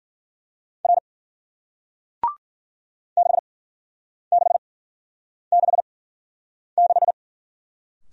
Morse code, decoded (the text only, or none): R 6666